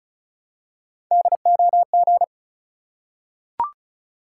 Morse code, heard as a keyed tone, about 36 wpm, 700 Hz. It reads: DOG
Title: Three Letter Words - 35wpm